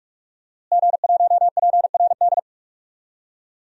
Morse code, heard as a keyed tone, about 41 wpm, 700 Hz. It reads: G1PRD
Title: DXCC-100 Call Signs - 5 Characters - Repeated 2x - 45wpm